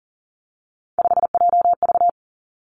Morse code, heard as a keyed tone, about 38 wpm, 700 Hz. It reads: 5JV
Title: Three Challenging Letter-Number Combinations 40wpm